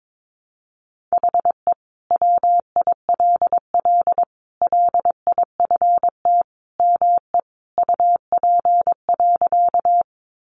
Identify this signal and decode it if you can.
Morse code, heard as a keyed tone, about 22 wpm, 700 Hz. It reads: HE WILL LIFT ME UP.
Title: Sentences from Top 1000 Words 22wpm